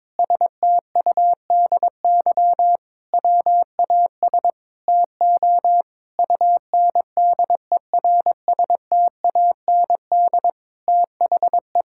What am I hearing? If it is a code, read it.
Morse code, heard as a keyed tone, about 22 wpm, 700 Hz. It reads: STUDY WAS TO UNDERSTAND THE
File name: Sentences with 7-Word Phrases - 22wpm